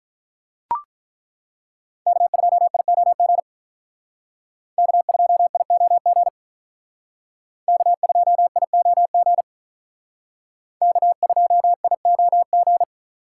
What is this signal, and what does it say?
Morse code, heard as a keyed tone, about 43 wpm, 700 Hz. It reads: K2IOG K2IOG K2IOG K2IOG
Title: Speed Racing - USA 1x3 Call Signs 35wpm